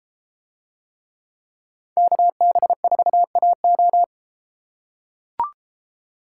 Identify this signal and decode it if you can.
Morse code, heard as a keyed tone, about 32 wpm, 700 Hz. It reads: KB4AO